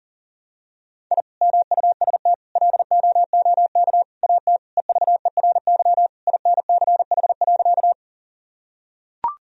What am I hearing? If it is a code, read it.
Morse code, heard as a keyed tone, about 38 wpm, 700 Hz. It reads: I MUST LOOK AT EVERY INCH.
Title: Sentences from Top 1000 Words - Review Part 3 - 40wpm